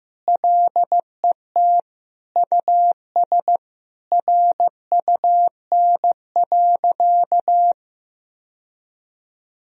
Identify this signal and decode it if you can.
Morse code, heard as a keyed tone, about 16 wpm, 700 Hz. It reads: LET US RUN.
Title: Sentences from Top 1000 Words - Review Part 1 - 15wpm